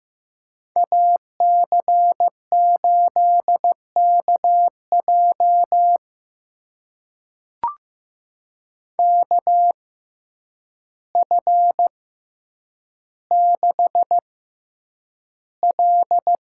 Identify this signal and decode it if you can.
Morse code, heard as a keyed tone, about 15 wpm, spaced as for 8 wpm, 700 Hz. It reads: AC8KJ KF6L
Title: USA Call Signs - 5 Characters - 6x Character Spacing - 15wpm